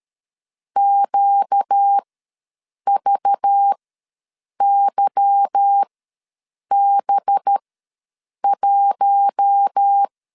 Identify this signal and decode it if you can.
Morse code, heard as a keyed tone, about 13 wpm, spaced as for 7 wpm, 800 Hz. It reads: QVYB1
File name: LEZIONI / 8 YJ45 / 8 Y ws 12wpm 3409